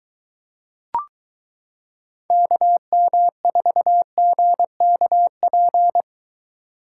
Morse code, heard as a keyed tone, about 22 wpm, 700 Hz. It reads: KM4GKP